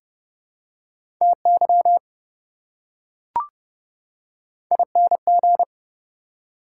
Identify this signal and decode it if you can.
Morse code, heard as a keyed tone, about 29 wpm, 700 Hz. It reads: TY ING
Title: Prefixes and Suffixes 30wpm